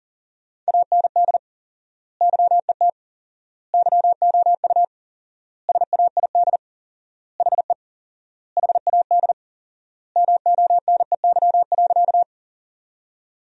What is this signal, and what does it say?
Morse code, heard as a keyed tone, about 40 wpm, 700 Hz. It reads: AND YET YOU SAID HE HAD MONEY.
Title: 4x Word-Spacing - Sentences from Top 500 Words - 40wpm